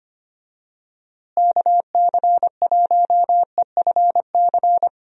Morse code, heard as a keyed tone, about 25 wpm, 700 Hz. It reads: KC1EFC
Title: USA Call Signs - 6 Characters - Repeated 3x - 25wpm